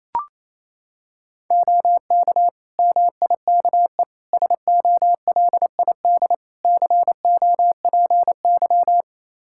Morse code, heard as a keyed tone, about 28 wpm, 700 Hz. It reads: OK MIKE SOLID COPY
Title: Practice QSO - Exchange 3 - More Info - 28wpm